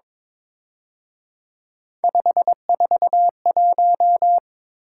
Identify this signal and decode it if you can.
Morse code, heard as a keyed tone, about 26 wpm, 700 Hz. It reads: 541
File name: Speed-Racing - Three Letter-Number Combinations - 22wpm